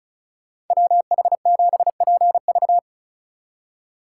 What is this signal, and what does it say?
Morse code, heard as a keyed tone, about 44 wpm, 700 Hz. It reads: WH7PV